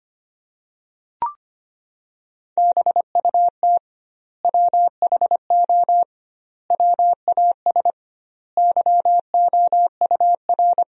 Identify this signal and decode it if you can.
Morse code, heard as a keyed tone, about 25 wpm, 700 Hz. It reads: BUT WHO WAS YOUR
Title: 2x Word-Spacing - Sentences from Top 200 Words - 25wpm